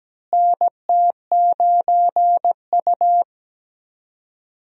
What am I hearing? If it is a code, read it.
Morse code, heard as a keyed tone, about 17 wpm, 700 Hz. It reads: NT9U
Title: USA Call Signs - 4 Characters - Repeated 3x - 17wpm